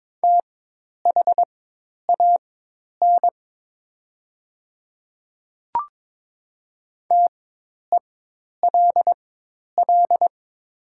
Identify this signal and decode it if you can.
Morse code, heard as a keyed tone, about 22 wpm, spaced as for 10 wpm, 700 Hz. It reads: THAN TELL
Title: 4x Character Spacing - Top 200 Words - 22wpm